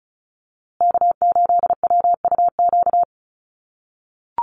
Morse code, heard as a keyed tone, about 34 wpm, 700 Hz. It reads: K8WUQ
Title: USA Call Signs Repeated 3x 35wpm